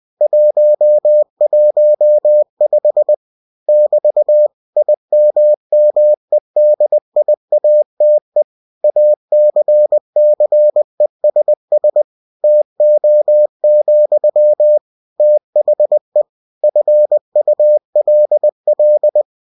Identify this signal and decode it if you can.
Morse code, heard as a keyed tone, about 20 wpm, 600 Hz. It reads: 115 <BT> IMMEDIATE ACCESS TO, THE FULL
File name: cw-books / winnie-the-pooh / winnie-20wpm-page-0115